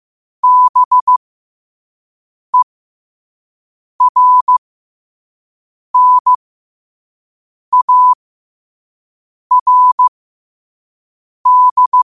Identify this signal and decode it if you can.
Morse code, heard as a keyed tone, about 15 wpm, spaced as for 5 wpm, 1000 Hz. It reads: BERNARD
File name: qotdfiles / 5wpm